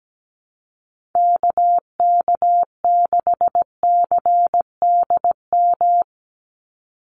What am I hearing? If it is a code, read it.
Morse code, heard as a keyed tone, about 17 wpm, 700 Hz. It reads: KK6CDM
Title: USA Call Signs 17wpm